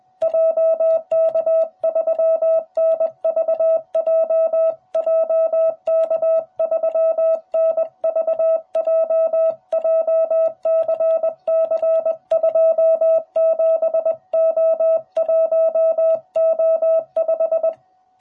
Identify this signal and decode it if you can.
Morse code, heard as a keyed tone, about 21 wpm, 660 Hz. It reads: JK3NVJ JK3NVJ JCC27O1O5